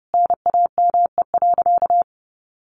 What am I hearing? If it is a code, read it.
Morse code, heard as a keyed tone, about 30 wpm, 700 Hz. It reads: NAME.
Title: Sentences from Top 1500 Words 30wpm